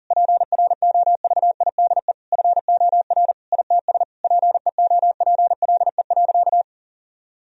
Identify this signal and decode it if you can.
Morse code, heard as a keyed tone, about 40 wpm, 700 Hz. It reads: PROVIDE FOR ITS PEOPLE.
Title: Sentences from Top 1000 Words - Review Part 3 - 40wpm